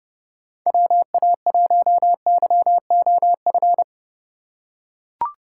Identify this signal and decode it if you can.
Morse code, heard as a keyed tone, about 32 wpm, 700 Hz. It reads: WA1YOF
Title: USA Call Signs - 6 Characters - 2x Character Spacing - 30wpm